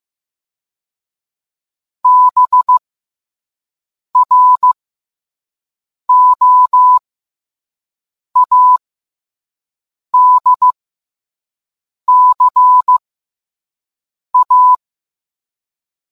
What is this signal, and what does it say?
Morse code, heard as a keyed tone, about 15 wpm, spaced as for 5 wpm, 1000 Hz. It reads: BROADCA